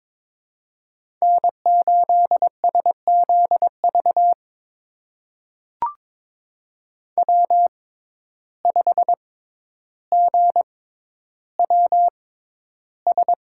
Morse code, heard as a keyed tone, about 22 wpm, spaced as for 11 wpm, 700 Hz. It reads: N8SZV W5GWS
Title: USA Call Signs - 5 Characters - 6x Character Spacing - 22wpm